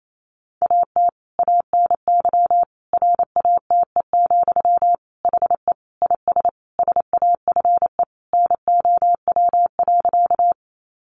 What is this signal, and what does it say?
Morse code, heard as a keyed tone, about 28 wpm, 700 Hz. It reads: AT ANY RATE, HE IS SAFE NOW.
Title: Sentences from Top 900 Words 28wpm